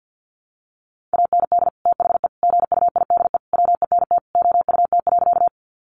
Morse code, heard as a keyed tone, about 50 wpm, 700 Hz. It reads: AND THE GUIDE WENT OUT.
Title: Sentences from Top 600 Words 50wpm